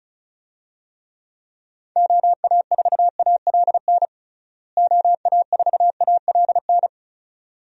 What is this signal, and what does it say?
Morse code, heard as a keyed tone, about 34 wpm, 700 Hz. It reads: OA4ALN OA4ALN
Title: USA Call Signs - 6 Characters - Repeated 2x - 35wpm